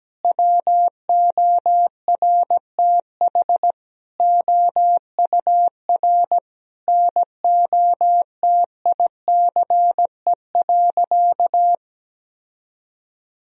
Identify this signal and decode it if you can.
Morse code, heard as a keyed tone, about 17 wpm, 700 Hz. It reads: WORTH OUR NOTICE.
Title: Sentences from Top 1000 Words - Review Part 2 - 17wpm